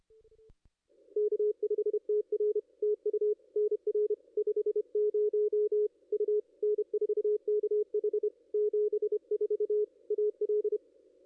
Morse code, heard as a keyed tone, about 29 wpm, 440 Hz. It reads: K5TR TU NR 50 U N4KH 74 AL
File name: contests / ss-cw-k5tr-op-n5rz -2200